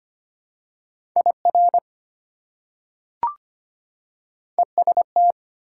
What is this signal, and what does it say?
Morse code, heard as a keyed tone, about 25 wpm, 700 Hz. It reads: IR EST
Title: Prefixes and Suffixes 25wpm